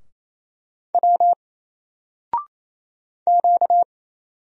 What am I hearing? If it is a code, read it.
Morse code, heard as a keyed tone, about 27 wpm, 700 Hz. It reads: W Q
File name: Single-Letter-Number-28wpm